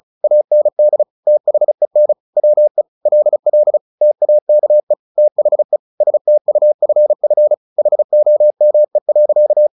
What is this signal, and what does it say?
Morse code, heard as a keyed tone, about 35 wpm, 600 Hz. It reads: AND THEN WE LL TAKE THE STUFF HOME.
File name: the old man and the sea / the old man and the sea 1 35wpm